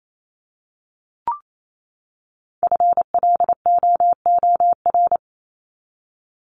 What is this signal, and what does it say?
Morse code, heard as a keyed tone, about 28 wpm, 700 Hz. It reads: FLOOR
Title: Top 1751-2000 Words 28wpm